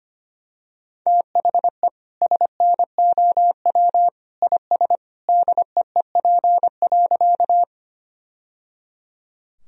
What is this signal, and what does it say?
Morse code, heard as a keyed tone, about 25 wpm, 700 Hz. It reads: THE SNOW IS DEEP.
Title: Sentences from Top 1000 Words - Review Part 3 - 25wpm